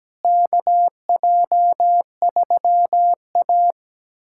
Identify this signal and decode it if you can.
Morse code, heard as a keyed tone, about 17 wpm, 700 Hz. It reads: KJ3A
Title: USA Callsign DE Callsign - 4 Characters - 17wpm